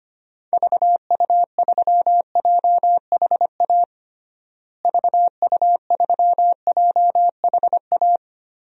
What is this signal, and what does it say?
Morse code, heard as a keyed tone, about 25 wpm, 700 Hz. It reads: VU3JHA VU3JHA
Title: DXCC-100 Call Signs - 6 Characters - Repeated 2x - 25wpm